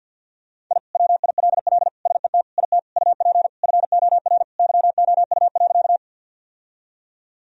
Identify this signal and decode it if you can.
Morse code, heard as a keyed tone, about 52 wpm, 700 Hz. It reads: I WILL SET IT UP FOR YOU.
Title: Sentences from Top 1000 Words - Review Part 1 - 50wpm